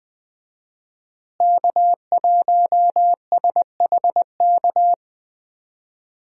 Morse code, heard as a keyed tone, about 20 wpm, 700 Hz. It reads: K1SHK
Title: USA Call Signs - 5 Characters - Repeated 2x - 20wpm